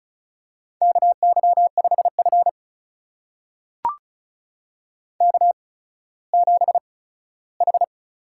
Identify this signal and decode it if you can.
Morse code, heard as a keyed tone, about 36 wpm, spaced as for 17 wpm, 700 Hz. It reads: KY5F K7H
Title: USA Call Signs - 4 Characters - 8x Character Spacing - 35wpm